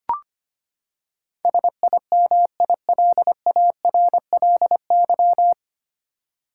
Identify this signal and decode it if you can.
Morse code, heard as a keyed tone, about 25 wpm, 700 Hz. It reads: SIMILARLY
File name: Nine Letter Words - 25wpm